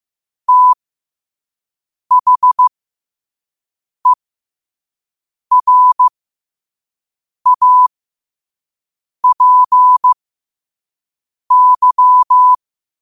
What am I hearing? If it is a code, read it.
Morse code, heard as a keyed tone, about 15 wpm, spaced as for 5 wpm, 1000 Hz. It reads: THERAPY
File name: qotdfiles / 5wpm